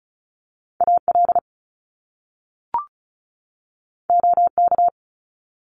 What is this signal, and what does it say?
Morse code, heard as a keyed tone, about 36 wpm, 700 Hz. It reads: AL OK
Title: Top Two Letter Combinations in Words 35wpm